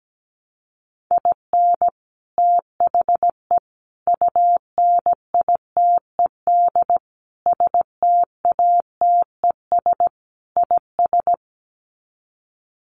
Morse code, heard as a keyed tone, about 17 wpm, 700 Hz. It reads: IN THE UNITED STATES IS
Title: Sets of 5 Words 17wpm